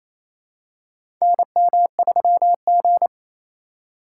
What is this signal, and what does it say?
Morse code, heard as a keyed tone, about 28 wpm, 700 Hz. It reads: NM3G
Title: USA Call Signs - 4 Characters - 2x Character Spacing - 28wpm